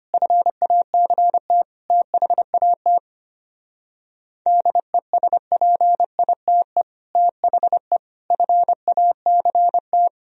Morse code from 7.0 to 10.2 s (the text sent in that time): THE FACT